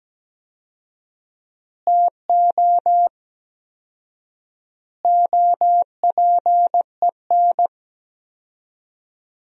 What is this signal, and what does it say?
Morse code, heard as a keyed tone, about 17 wpm, 700 Hz. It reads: TO OPEN
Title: 4x Word-Spacing - Sentences from Top 300 Words - 17wpm